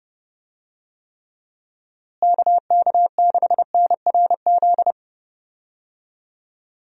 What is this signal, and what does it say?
Morse code, heard as a keyed tone, about 30 wpm, 700 Hz. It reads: KK6NRZ
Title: USA Call Signs - 6 Characters - Repeated 2x - 30wpm